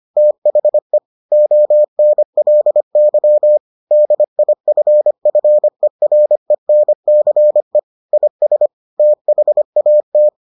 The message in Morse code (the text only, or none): THE ONLY DIFFERENCE IS THAT